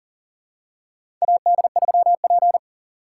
Morse code, heard as a keyed tone, about 40 wpm, 700 Hz. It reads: AD3P
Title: USA Call Signs - 4 Characters - 3x Character Spacing - 40wpm